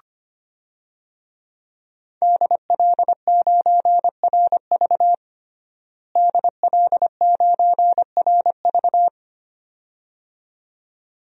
Morse code, heard as a keyed tone, about 25 wpm, 700 Hz. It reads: DL9RV DL9RV
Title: DXCC-100 Call Signs - 5 Characters - Repeated 2x - 25wpm